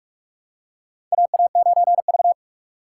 Morse code, heard as a keyed tone, about 41 wpm, 700 Hz. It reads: AA9V
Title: USA Call Signs - 4 Characters - 2x Character Spacing - 45wpm